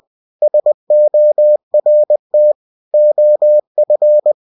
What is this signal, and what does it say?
Morse code, heard as a keyed tone, about 20 wpm, 600 Hz. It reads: SORT OF
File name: cw-books / winnie-the-pooh / winnie-20wpm-page-0009